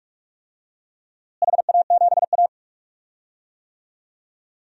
Morse code, heard as a keyed tone, about 46 wpm, 700 Hz. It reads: HA7A
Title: DXCC-100 Call Signs - 4 Characters - Repeated 3x - 45wpm